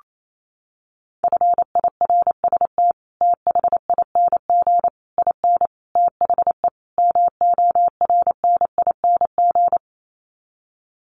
Morse code, heard as a keyed tone, about 28 wpm, 700 Hz. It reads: FIRST THING IN THE MORNING
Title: Sets of 5 Words Encore2 28wpm